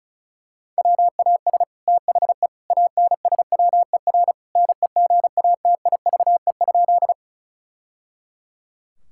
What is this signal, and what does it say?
Morse code, heard as a keyed tone, about 36 wpm, 700 Hz. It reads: WAS THE ANSWER NEGATIVE?
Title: Sentences from Top 1000 Words - Review Part 6 - 35wpm